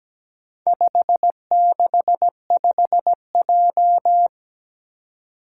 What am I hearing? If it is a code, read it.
Morse code, heard as a keyed tone, about 17 wpm, 700 Hz. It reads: 565J
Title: Four Challenging Letter-Number Combinations 17wpm